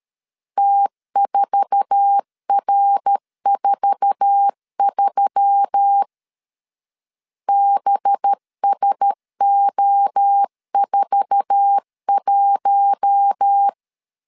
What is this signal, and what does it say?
Morse code, heard as a keyed tone, about 13 wpm, 800 Hz. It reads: T4R43 BSO41
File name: LEZIONI / 8 YJ45 / 8 YJ4 12wpm 2151